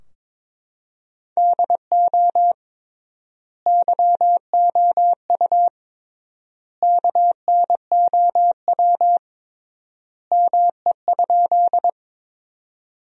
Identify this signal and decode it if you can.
Morse code, heard as a keyed tone, about 22 wpm, 700 Hz. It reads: DO YOU KNOW ME?
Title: 3x Word-Spacing - Sentences from Top 200 Words - 22wpm